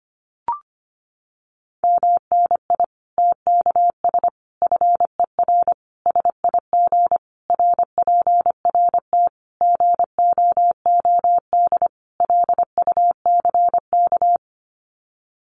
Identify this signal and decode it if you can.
Morse code, heard as a keyed tone, about 25 wpm, 700 Hz. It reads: MNI TKS FER SIG RPRT GOOD LUCK
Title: Common QSO Phrases 25wpm